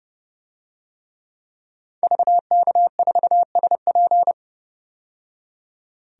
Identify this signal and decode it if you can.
Morse code, heard as a keyed tone, about 30 wpm, 700 Hz. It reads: VK4SP
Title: DXCC-100 Call Signs - 5 Characters - 30wpm